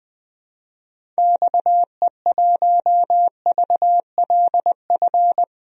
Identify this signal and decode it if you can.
Morse code, heard as a keyed tone, about 20 wpm, 700 Hz. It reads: XE1VLF
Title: DXCC-100 Call Signs - 6 Characters - 20wpm